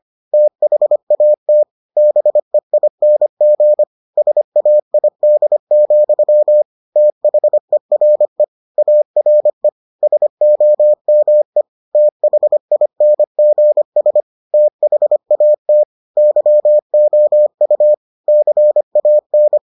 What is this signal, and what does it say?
Morse code, heard as a keyed tone, about 25 wpm, 600 Hz. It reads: THAT BEING SAID, THERE ARE SOME THINGS THAT YOU CAN